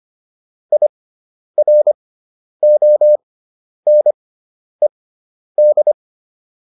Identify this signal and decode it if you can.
Morse code, heard as a keyed tone, about 25 wpm, spaced as for 10 wpm, 600 Hz. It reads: IRONED
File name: IR6 10